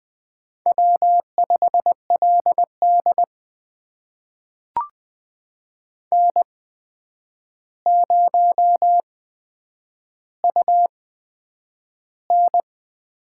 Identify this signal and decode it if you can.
Morse code, heard as a keyed tone, about 20 wpm, spaced as for 9 wpm, 700 Hz. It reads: W5LD N0UN
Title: USA Call Signs - 4 Characters - 8x Character Spacing - 20wpm